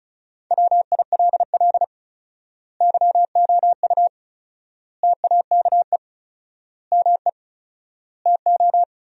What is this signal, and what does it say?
Morse code, heard as a keyed tone, about 36 wpm, 700 Hz. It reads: WILL YOU TAKE ME TO
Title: 4x Word-Spacing - Sentences from Top 200 Words - 35wpm